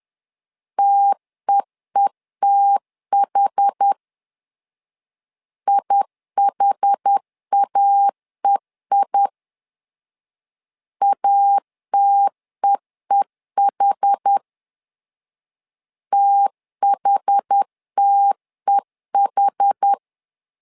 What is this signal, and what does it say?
Morse code, heard as a keyed tone, about 10 wpm, 800 Hz. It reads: TEETH IHAEI ATEEH THTEH
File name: LEZIONI / 2 UHDN / 2 UH 10wpm 1714